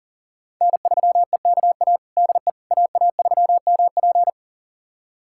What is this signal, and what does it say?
Morse code, heard as a keyed tone, about 37 wpm, 700 Hz. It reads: N3EKA DE AA3MP
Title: USA Callsign DE Callsign - 5 Characters - 40wpm